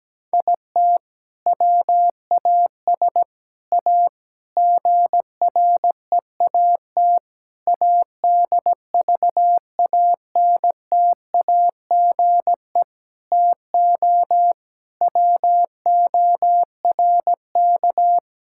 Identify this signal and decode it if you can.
Morse code, heard as a keyed tone, about 17 wpm, 700 Hz. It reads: IT WAS A GREAT ADVANTAGE TO WORK
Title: Sentences from Top 1000 Words - Review Part 6 - 17wpm